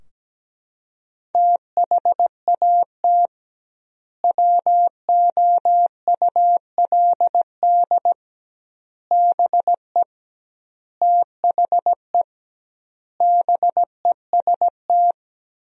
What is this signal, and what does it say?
Morse code, heard as a keyed tone, about 17 wpm, 700 Hz. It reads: THAT WOULD BE THE BEST
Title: 2x Word-Spacing - Sentences from Top 400 Words - 17wpm